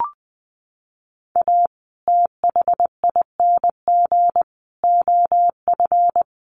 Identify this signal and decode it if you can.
Morse code, heard as a keyed tone, about 19 wpm, 700 Hz. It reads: A THING OF
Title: Sets of 5 Words Encore 20wpm